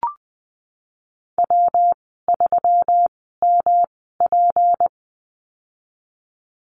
Morse code, heard as a keyed tone, about 20 wpm, spaced as for 15 wpm, 700 Hz. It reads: W3MP
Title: 154 Lesson 25g - Letter p - Call Signs 20wpm